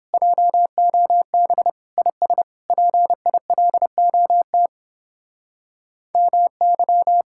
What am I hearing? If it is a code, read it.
Morse code, heard as a keyed tone, about 29 wpm, 700 Hz. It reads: JOB IS PILOT MY